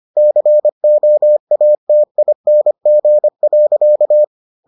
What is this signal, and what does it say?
Morse code, heard as a keyed tone, about 25 wpm, 600 Hz. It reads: COATING.